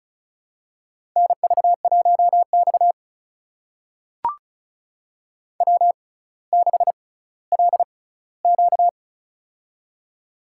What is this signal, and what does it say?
Morse code, heard as a keyed tone, about 36 wpm, spaced as for 18 wpm, 700 Hz. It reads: NV1X W6LQ
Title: USA Call Signs - 4 Characters - 6x Character Spacing - 35wpm